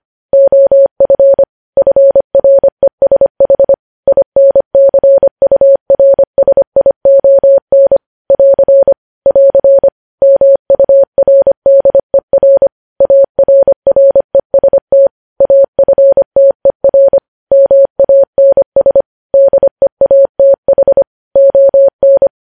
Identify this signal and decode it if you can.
Morse code, heard as a keyed tone, about 25 wpm, 550 Hz. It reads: OF FRESH INCURSION <AR> <AR> MURDER ARREST AFTER MANS DEATH ON
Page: News Headlines in Morse Code at 25 WPM